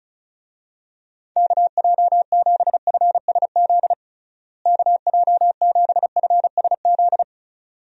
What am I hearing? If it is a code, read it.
Morse code, heard as a keyed tone, about 34 wpm, 700 Hz. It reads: KJ7FSZ KJ7FSZ
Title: USA Call Signs - 6 Characters - Repeated 2x - 35wpm